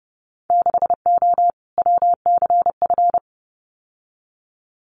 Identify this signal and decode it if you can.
Morse code, heard as a keyed tone, about 30 wpm, 700 Hz. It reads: 6O WCF
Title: Winter Field Day Exchanges Encore 30wpm